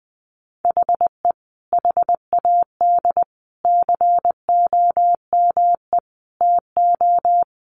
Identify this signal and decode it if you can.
Morse code, heard as a keyed tone, about 20 wpm, 700 Hz. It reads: HE HAD COME TO
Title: Sentences from Top 1500 Words 20wpm